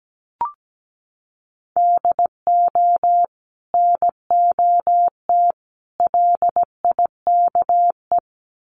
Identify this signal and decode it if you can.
Morse code, heard as a keyed tone, about 17 wpm, 700 Hz. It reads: DO NOT LIKE
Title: Sets of 3 Words 17wpm